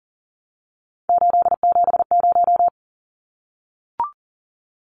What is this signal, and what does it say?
Morse code, heard as a keyed tone, about 40 wpm, 700 Hz. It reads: 870
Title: Numbers - 3-Digit 40wpm